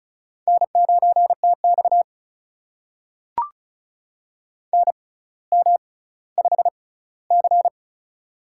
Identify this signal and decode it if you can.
Morse code, heard as a keyed tone, about 36 wpm, spaced as for 18 wpm, 700 Hz. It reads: N9TX NM5C
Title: USA Call Signs - 4 Characters - 6x Character Spacing - 35wpm